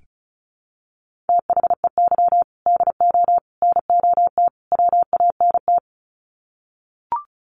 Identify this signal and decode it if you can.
Morse code, heard as a keyed tone, about 34 wpm, 700 Hz. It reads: THEY DO NOT WANT
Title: Sets of 4 Words 35wpm